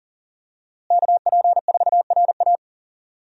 Morse code, heard as a keyed tone, about 40 wpm, 700 Hz. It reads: KP4RA
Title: DXCC-100 Call Signs - 5 Characters - Repeated 3x - 40wpm